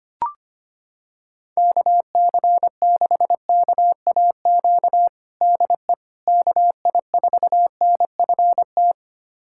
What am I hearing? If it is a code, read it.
Morse code, heard as a keyed tone, about 25 wpm, 700 Hz. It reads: KC6KAQ DE KI4NFT
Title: USA Callsign DE Callsign - 6 Characters - 25wpm